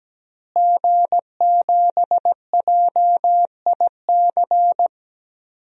Morse code, heard as a keyed tone, about 17 wpm, 700 Hz. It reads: G7JIC
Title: DXCC-100 Call Signs - 5 Characters - Repeated 2x - 17wpm